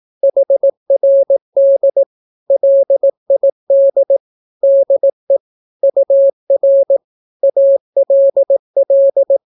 Morse code, heard as a keyed tone, about 18 wpm, 550 Hz. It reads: HRD LID DE UR ALL
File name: naukapodstaw / skroty / skroty